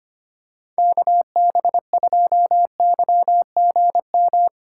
Morse code, heard as a keyed tone, about 25 wpm, 700 Hz. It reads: KB2YGM
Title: USA Call Signs - 6 Characters - 3x Character Spacing - 25wpm